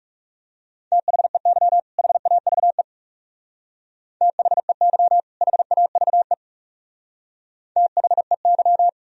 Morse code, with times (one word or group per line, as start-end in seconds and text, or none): 0.9-1.8 THEY
2.0-2.8 HAVE
4.2-5.2 THEY
5.4-6.3 HAVE
7.8-8.9 THEY